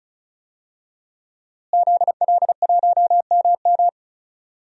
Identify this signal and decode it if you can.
Morse code, heard as a keyed tone, about 36 wpm, 700 Hz. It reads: ZL1MM
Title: DXCC-100 Call Signs - 5 Characters - Repeated 3x - 35wpm